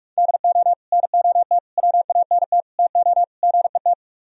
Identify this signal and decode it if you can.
Morse code, heard as a keyed tone, about 44 wpm, 700 Hz. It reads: DO NOT WANT TO GET